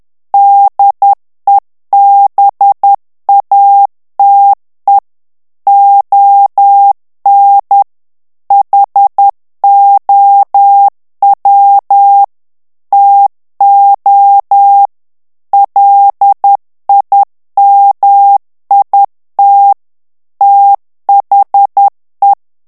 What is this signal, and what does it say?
Morse code, heard as a keyed tone, about 10 wpm, 800 Hz. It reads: DEBATE ON HOW TO LIMIT THE
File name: morse mp3 / ANGLAIS texte clair 2 vitesse 10 mots minute